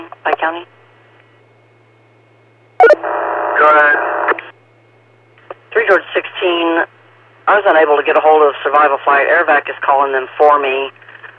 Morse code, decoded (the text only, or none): none